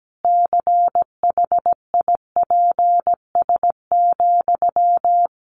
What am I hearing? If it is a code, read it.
Morse code, heard as a keyed tone, about 17 wpm, 700 Hz. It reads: CHIPS,
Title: Easy Phrases 17wpm